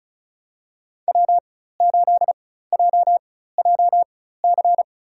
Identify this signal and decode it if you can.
Morse code, heard as a keyed tone, about 34 wpm, spaced as for 16 wpm, 700 Hz. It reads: W8JJC
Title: USA Call Signs - 5 Characters - 4x Character Spacing - 35wpm